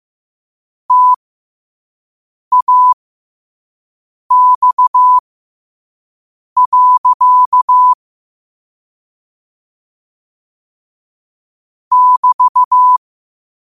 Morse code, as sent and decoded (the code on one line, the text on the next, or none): - .- -..- .-.-.- / -...-
TAX. <BT>